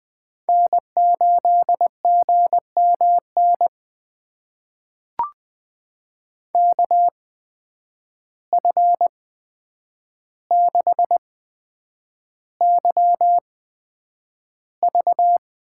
Morse code, heard as a keyed tone, about 20 wpm, spaced as for 9 wpm, 700 Hz. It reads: N8GMN KF6YV
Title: USA Call Signs - 5 Characters - 8x Character Spacing - 20wpm